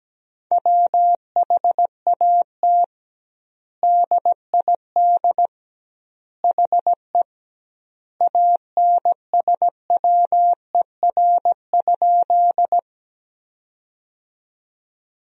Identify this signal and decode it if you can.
Morse code, heard as a keyed tone, about 17 wpm, 700 Hz. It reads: WHAT DID HE ANSWER?
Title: 2x Word-Spacing - Sentences from Top 300 Words - 17wpm